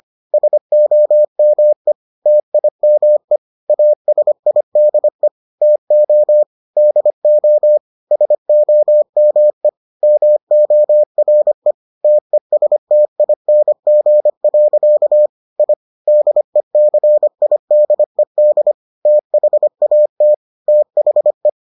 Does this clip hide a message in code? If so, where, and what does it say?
Yes: Morse code from 0.3 to 21.5 s, heard as a keyed tone, about 25 wpm, 600 Hz. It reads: SOME TIME ASIDE TO DO SOME MORE TESTING. I DECIDED THAT THE